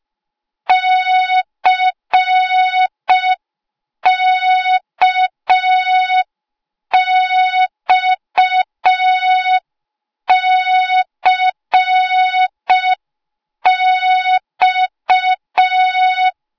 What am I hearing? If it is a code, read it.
Morse code, heard as a keyed tone, about 5 wpm, 3700 Hz. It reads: CKXCX